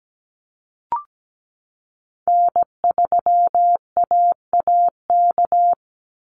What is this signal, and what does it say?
Morse code, heard as a keyed tone, about 17 wpm, 700 Hz. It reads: N3AAK
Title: Call Signs Repeated 3x - 17wpm